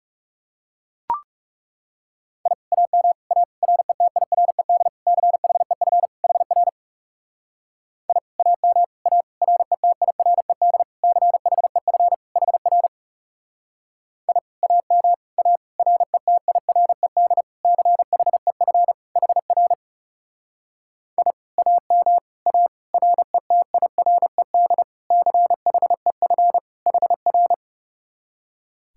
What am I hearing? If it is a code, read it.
Morse code, heard as a keyed tone, about 37 wpm, 700 Hz. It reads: I AM A RETIRED CHEF HR I AM A RETIRED CHEF HR I AM A RETIRED CHEF HR I AM A RETIRED CHEF HR
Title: Speed-Racing Common QSO Phrases - 30wpm